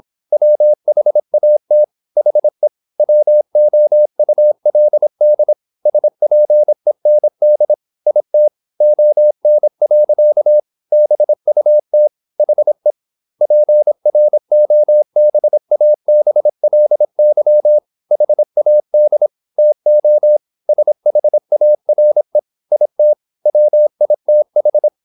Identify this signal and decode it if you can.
Morse code, heard as a keyed tone, about 26 wpm, 600 Hz. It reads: WHAT HE WOULD SPEND IT ON. BUT HE PROBABLY HAD TO SHARE IT WITH
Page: Never 26wpm | CW med Gnister